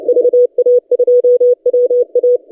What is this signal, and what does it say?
Morse code, heard as a keyed tone, about 30 wpm, 490 Hz. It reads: VA2WA
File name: posts / 3d2rh-20cw- -0330z